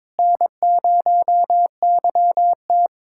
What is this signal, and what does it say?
Morse code, heard as a keyed tone, about 22 wpm, 700 Hz. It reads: N0YT
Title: USA Callsign DE Callsign - 4 Characters - 22wpm